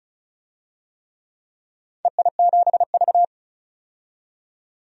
Morse code, heard as a keyed tone, about 36 wpm, 700 Hz. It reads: EI7V
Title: DXCC-100 Call Signs - 4 Characters - 35wpm